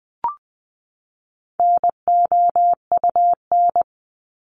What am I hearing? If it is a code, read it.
Morse code, heard as a keyed tone, about 20 wpm, 700 Hz. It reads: NOUN
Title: Top 500 Words 20wpm